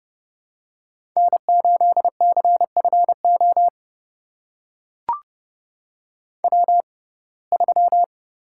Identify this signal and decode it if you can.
Morse code, heard as a keyed tone, about 32 wpm, spaced as for 22 wpm, 700 Hz. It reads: N8CFO W3
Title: USA Call Signs - 5 Characters - 6x Character Spacing - 30wpm